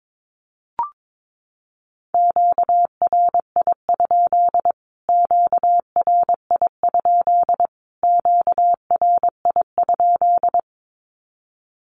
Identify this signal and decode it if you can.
Morse code, heard as a keyed tone, about 22 wpm, 700 Hz. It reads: QRI? QRI? QRI?
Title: Q-Codes Repeated 3x 22wpm